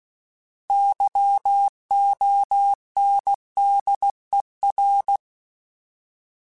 Morse code, heard as a keyed tone, about 16 wpm, 800 Hz. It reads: YONDER
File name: Alphabetical / Alphabetical mp3s / Ys @15